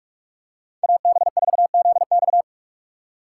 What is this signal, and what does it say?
Morse code, heard as a keyed tone, about 46 wpm, 700 Hz. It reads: AB4ZX
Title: USA Call Signs - 5 Characters - 2x Character Spacing - 45wpm